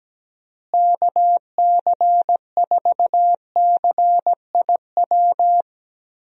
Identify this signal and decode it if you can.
Morse code, heard as a keyed tone, about 17 wpm, 700 Hz. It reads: KC4CIW
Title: USA Call Signs - 6 Characters - Repeated 2x - 17wpm